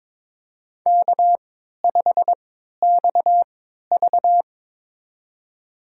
Morse code, heard as a keyed tone, about 21 wpm, spaced as for 12 wpm, 700 Hz. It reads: K5XV
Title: USA Call Signs - 4 Characters - 3x Character Spacing - 22wpm